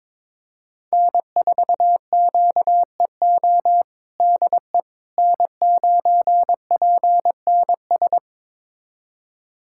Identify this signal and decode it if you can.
Morse code, heard as a keyed tone, about 21 wpm, 700 Hz. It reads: N4QEO DE N9PNS
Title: USA Callsign DE Callsign - 5 Characters - 22wpm